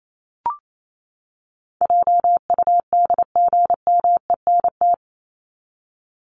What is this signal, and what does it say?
Morse code, heard as a keyed tone, about 28 wpm, 700 Hz. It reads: JUDGMENT
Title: Top 1751-2000 Words 28wpm